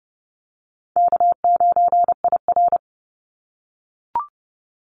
Morse code, heard as a keyed tone, about 30 wpm, 700 Hz. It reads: K9IR
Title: Call Signs - 30wpm